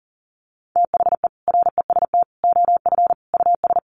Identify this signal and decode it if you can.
Morse code, heard as a keyed tone, about 40 wpm, 700 Hz. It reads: THE REST OF US
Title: Sets of 4 Words 40wpm